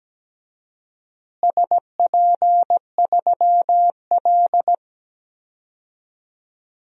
Morse code, heard as a keyed tone, about 17 wpm, 700 Hz. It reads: SP3L
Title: DXCC-100 Call Signs - 4 Characters - Repeated 2x - 17wpm